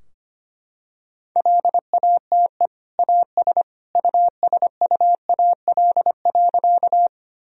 Morse code, heard as a keyed tone, about 25 wpm, 700 Hz. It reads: LATE AS USUAL.
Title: Sentences from Top 1000 Words - Review Part 2 - 25wpm